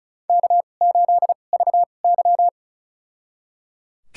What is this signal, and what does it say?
Morse code, heard as a keyed tone, about 36 wpm, spaced as for 25 wpm, 700 Hz. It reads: K8VY
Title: USA Call Signs - 4 Characters - 2x Character Spacing - 35wpm